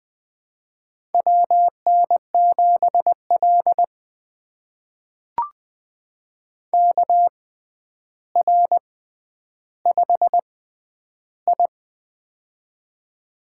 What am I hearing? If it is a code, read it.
Morse code, heard as a keyed tone, about 20 wpm, spaced as for 10 wpm, 700 Hz. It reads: WN7L KR5I
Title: USA Call Signs - 4 Characters - 6x Character Spacing - 20wpm